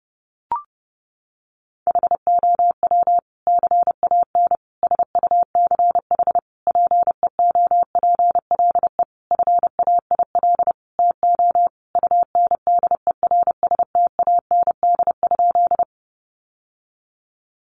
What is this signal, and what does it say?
Morse code, heard as a keyed tone, about 30 wpm, 700 Hz. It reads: HOW CAN SUCH PEOPLE FAIL TO UNDERSTAND?
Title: Sentences from Top 1100 Words 30wpm